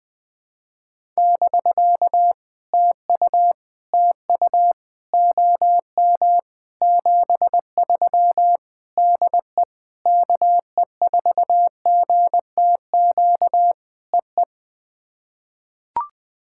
Morse code, heard as a keyed tone, about 20 wpm, 700 Hz. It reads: <BK> TU TU OM 73 DE KE4GTQ EE
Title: POTA QSO Practice - 20wpm